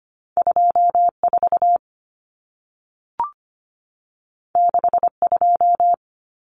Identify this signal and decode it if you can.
Morse code, heard as a keyed tone, about 25 wpm, 700 Hz. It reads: 24 62
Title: Common Cut Numbers - 2-Digit 25wpm